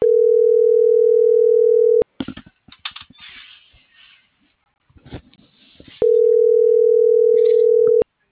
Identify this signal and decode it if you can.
Ringback tone (North America), 440 + 480 Hz.